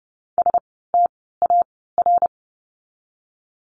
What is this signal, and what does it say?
Morse code, heard as a keyed tone, about 30 wpm, 700 Hz. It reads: S T A R
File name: Lesson 16f - Letter c - Words 30wpm